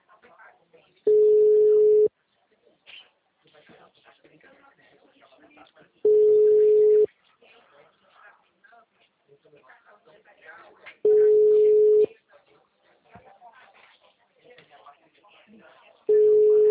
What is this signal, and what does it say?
Ringback tone (Europe, 425 Hz), with speech.